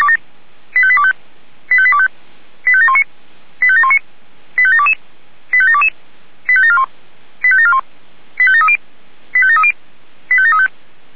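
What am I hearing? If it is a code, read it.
Five-tone selective call (ZVEI), heard as quick runs of five tones, tones of about 70 ms. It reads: 85825 85825 85719 85719 85722 85722 85741 85741 85730 85730 85735
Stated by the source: ZVEI five-tone selective call received on an SDR.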